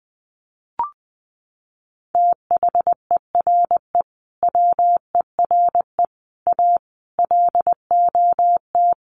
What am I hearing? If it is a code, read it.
Morse code, heard as a keyed tone, about 20 wpm, 700 Hz. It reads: THERE WERE A LOT
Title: Sets of 5 Words 20wpm